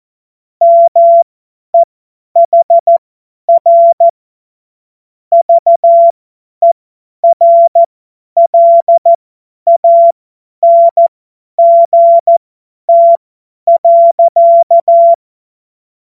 Morse code, heard as a keyed tone, about 14 wpm, spaced as for 10 wpm, 680 Hz. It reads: MEHR VERLANGT.
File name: Witz-22-Juli-2025-50-F70